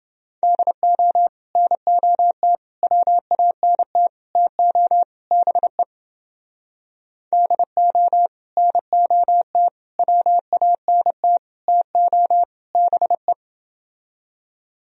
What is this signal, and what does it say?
Morse code, heard as a keyed tone, about 31 wpm, 700 Hz. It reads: DO NOT WANT TO BE DO NOT WANT TO BE